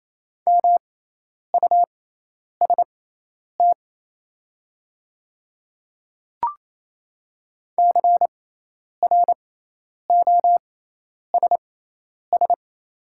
Morse code, heard as a keyed tone, about 28 wpm, spaced as for 10 wpm, 700 Hz. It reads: MUST CROSS
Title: 6x Character Spacing - Top 500 Words - 28wpm